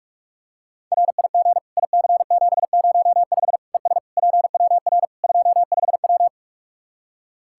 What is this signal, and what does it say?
Morse code, heard as a keyed tone, about 46 wpm, 700 Hz. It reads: RIG IC705 ES PWR 25W
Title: Practice QSO - Exchange 2 - Info - 45wpm